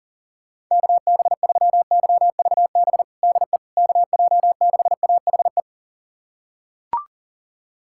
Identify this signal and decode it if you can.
Morse code, heard as a keyed tone, about 40 wpm, 700 Hz. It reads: KB3YVB DE KJ6AHE
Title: USA Callsign DE Callsign - 6 Characters - 40wpm